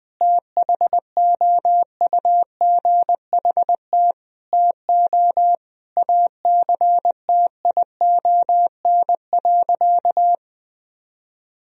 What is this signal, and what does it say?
Morse code, heard as a keyed tone, about 21 wpm, 700 Hz. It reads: THOUGHT TO ACTION.
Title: Sentences from Top 1000 Words - Review Part 4 - 20wpm